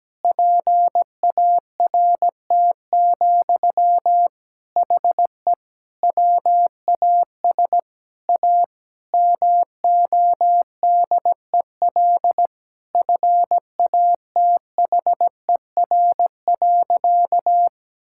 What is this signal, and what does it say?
Morse code, heard as a keyed tone, about 17 wpm, 700 Hz. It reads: PART, HE WAS A MODEL FATHER.
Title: Sentences from Top 1000 Words - Review Part 5 - 17wpm